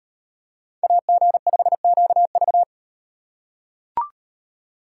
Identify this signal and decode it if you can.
Morse code, heard as a keyed tone, about 38 wpm, 700 Hz. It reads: AG5QV